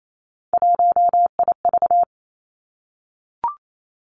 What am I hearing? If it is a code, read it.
Morse code, heard as a keyed tone, about 28 wpm, 690 Hz. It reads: 1IV
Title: Three Challenging Letter-Number Combinations 28wpm